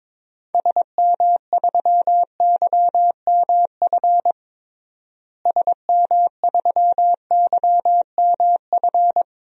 Morse code, heard as a keyed tone, about 22 wpm, 700 Hz. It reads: SM3YMF SM3YMF
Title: DXCC-100 Call Signs - 6 Characters - Repeated 2x - 22wpm